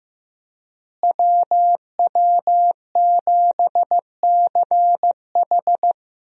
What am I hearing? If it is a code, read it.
Morse code, heard as a keyed tone, about 15 wpm, 700 Hz. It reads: WW7CH
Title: USA Call Signs - 5 Characters - Repeated 2x - 15wpm